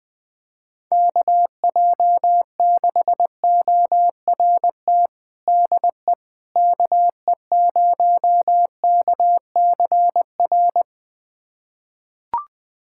Morse code, heard as a keyed tone, about 20 wpm, 700 Hz. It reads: KJ6ORT DE KE0KCR
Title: USA Callsign DE Callsign - 6 Characters - 20wpm